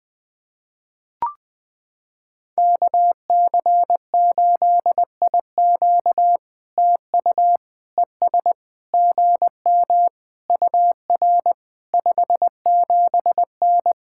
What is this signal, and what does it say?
Morse code, heard as a keyed tone, about 21 wpm, 700 Hz. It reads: KC8IQ TU ES GM UR 57N
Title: POTA QSO Practice - 20wpm